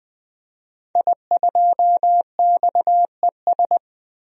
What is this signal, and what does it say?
Morse code, heard as a keyed tone, about 20 wpm, 700 Hz. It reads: I2XES
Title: DXCC-100 Call Signs - 5 Characters - 20wpm